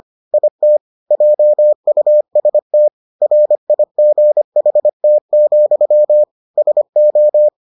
Morse code, heard as a keyed tone, about 25 wpm, 600 Hz. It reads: IT JUST RIGHT, SO